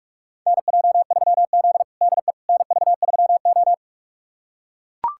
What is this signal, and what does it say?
Morse code, heard as a keyed tone, about 42 wpm, 700 Hz. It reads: NJ3Z DE NV3O
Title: USA Callsign DE Callsign - 4 Characters - 45wpm